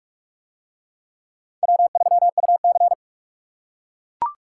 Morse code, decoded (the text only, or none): W3UC